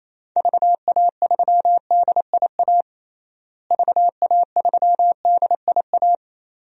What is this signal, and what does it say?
Morse code, heard as a keyed tone, about 27 wpm, 700 Hz. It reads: VA3DIA VA3DIA
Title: DXCC-100 Call Signs - 6 Characters - Repeated 2x - 28wpm